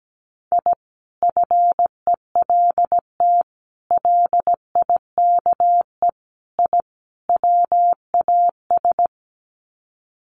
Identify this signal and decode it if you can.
Morse code, heard as a keyed tone, about 17 wpm, 700 Hz. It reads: I FELT LIKE I WAS
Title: Sets of 5 Words Encore 17wpm